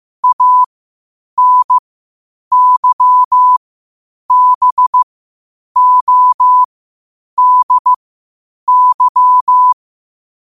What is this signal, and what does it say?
Morse code, heard as a keyed tone, about 15 wpm, spaced as for 8 wpm, 1000 Hz. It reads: ANYBODY
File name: qotdfiles / 8wpm